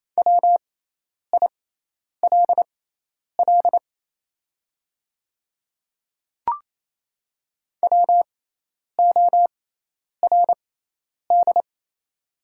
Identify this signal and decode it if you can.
Morse code, heard as a keyed tone, about 27 wpm, spaced as for 10 wpm, 700 Hz. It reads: WILL WORD
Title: 6x Character Spacing - Top 300 Words - 28wpm